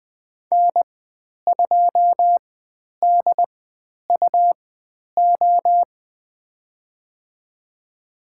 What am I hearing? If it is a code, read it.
Morse code, heard as a keyed tone, about 20 wpm, spaced as for 10 wpm, 700 Hz. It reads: N2DUO
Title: 102 Lesson 17g - Letter u - Call Signs 20wpm